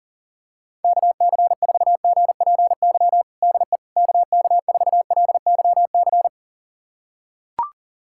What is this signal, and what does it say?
Morse code, heard as a keyed tone, about 40 wpm, 700 Hz. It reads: KC4GPY DE KK4LYC
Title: USA Callsign DE Callsign - 6 Characters - 40wpm